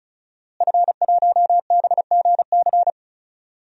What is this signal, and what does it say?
Morse code, heard as a keyed tone, about 36 wpm, 700 Hz. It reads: F1BGC
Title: DXCC-100 Call Signs - 5 Characters - Repeated 3x - 35wpm